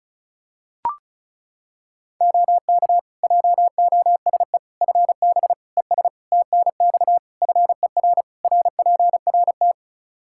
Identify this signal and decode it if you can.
Morse code, heard as a keyed tone, about 36 wpm, 700 Hz. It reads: OK JOSE FB ES TNX FER RPRT
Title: K1USN SST QSO Practice 35wpm